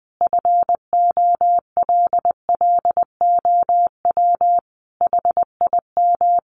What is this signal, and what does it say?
Morse code, heard as a keyed tone, about 19 wpm, 700 Hz. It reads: FOLLOW HIM
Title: Sentences from Top 700 Words 20wpm